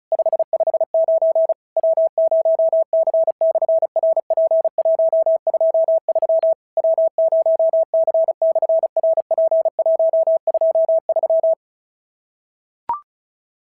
Morse code, heard as a keyed tone, about 36 wpm, 650 Hz. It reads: 559 W0C/RP123 W0C/RP123
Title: SOTA QSO Practice 35wpm